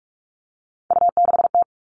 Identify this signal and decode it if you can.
Morse code, heard as a keyed tone, about 44 wpm, 700 Hz. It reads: U6T
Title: All Cut Numbers - 3-Digit 45wpm